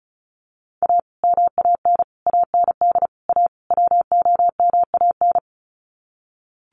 Morse code, heard as a keyed tone, about 34 wpm, 700 Hz. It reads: A MAN AND A WOMAN
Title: Sets of 5 Words Encore 35wpm